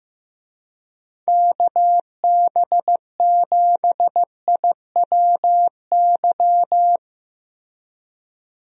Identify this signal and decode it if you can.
Morse code, heard as a keyed tone, about 15 wpm, 700 Hz. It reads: KB7IWY
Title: USA Call Signs - 6 Characters - Repeated 3x - 15wpm